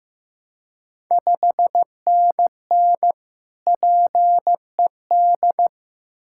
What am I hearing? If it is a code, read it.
Morse code, heard as a keyed tone, about 15 wpm, 700 Hz. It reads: 5NN PED
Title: Ontario QSO Party Exchange 15wpm